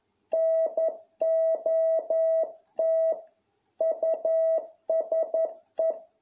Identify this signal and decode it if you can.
Morse code, heard as a keyed tone, about 11 wpm, 650 Hz. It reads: NOT USE